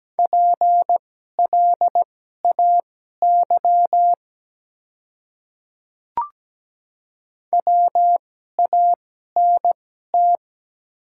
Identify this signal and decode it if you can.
Morse code, heard as a keyed tone, about 17 wpm, spaced as for 12 wpm, 690 Hz. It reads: PLAY WANT
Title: 2x Character Spacing - Top 300 Words - 17wpm